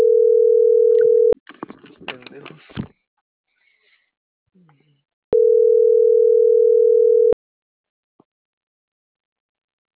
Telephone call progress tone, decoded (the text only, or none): ringback tone